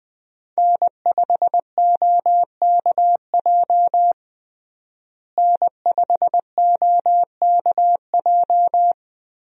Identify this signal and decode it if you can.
Morse code, heard as a keyed tone, about 20 wpm, 700 Hz. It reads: N5OKJ N5OKJ
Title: USA Call Signs - 5 Characters - Repeated 2x - 20wpm